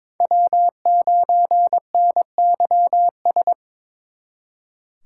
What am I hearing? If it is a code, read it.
Morse code, heard as a keyed tone, about 26 wpm, 700 Hz. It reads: W9NYS
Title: Speed Racing - USA 1x3 Call Signs 22wpm